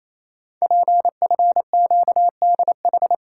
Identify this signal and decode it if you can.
Morse code, heard as a keyed tone, about 27 wpm, 700 Hz. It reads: PFQDH
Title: Code Groups - 5 Letters - 28wpm